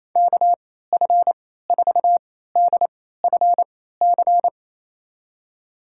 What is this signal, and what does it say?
Morse code, heard as a keyed tone, about 28 wpm, spaced as for 16 wpm, 700 Hz. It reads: KF4DFC
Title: USA Call Signs - 6 Characters - 3x Character Spacing - 28wpm